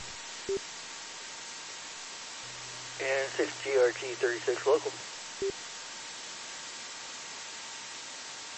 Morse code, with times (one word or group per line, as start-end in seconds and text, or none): none